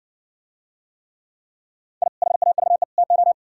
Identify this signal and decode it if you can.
Morse code, heard as a keyed tone, about 59 wpm, 700 Hz. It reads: I HAVE TO